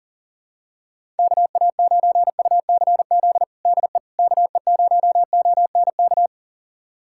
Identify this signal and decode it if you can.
Morse code, heard as a keyed tone, about 40 wpm, 700 Hz. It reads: KA9UCZ DE KE0ONK
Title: USA Callsign DE Callsign - 6 Characters - 40wpm